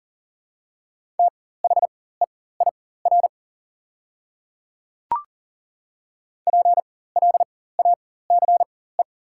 Morse code, heard as a keyed tone, about 40 wpm, spaced as for 19 wpm, 690 Hz. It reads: THEIR PLACE
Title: 4x Character Spacing - Top 200 Words - 40wpm